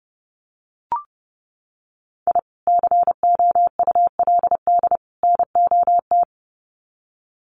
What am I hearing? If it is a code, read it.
Morse code, heard as a keyed tone, about 30 wpm, 700 Hz. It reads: I COULD NOT
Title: Sets of 3 Words 30wpm